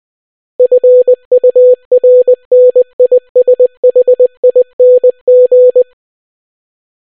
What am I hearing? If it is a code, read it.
Morse code, heard as a keyed tone, about 20 wpm, 500 Hz. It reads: FURNISHING